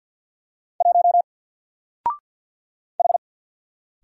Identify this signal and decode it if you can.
Morse code, heard as a keyed tone, about 50 wpm, 700 Hz. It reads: 1 H
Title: Single Character (Letter:Number:Punctuation:Prosign) 50wpm